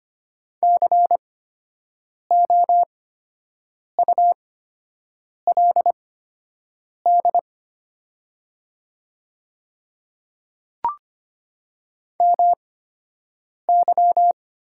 Morse code, heard as a keyed tone, about 25 wpm, spaced as for 7 wpm, 700 Hz. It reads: COULD MY
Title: 8x Character Spacing - Top 100 Words - 25wpm